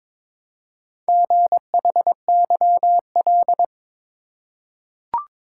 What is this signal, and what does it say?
Morse code, heard as a keyed tone, about 22 wpm, 700 Hz. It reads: GHYL
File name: practice / 22wpm / Four Letter-Number Combinations - 22wpm